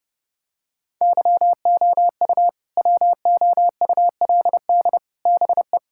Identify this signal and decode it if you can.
Morse code, heard as a keyed tone, about 32 wpm, 700 Hz. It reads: YOU WOULD BE
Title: Sentences from Top 1000 Words - Review Part 6 - 30wpm